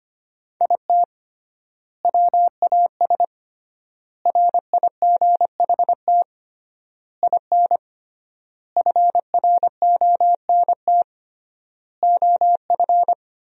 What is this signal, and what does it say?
Morse code, heard as a keyed tone, about 25 wpm, 700 Hz. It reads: IT WAS RIGHT IN FRONT OF
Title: 3x Word-Spacing - Sentences from Top 500 Words - 25wpm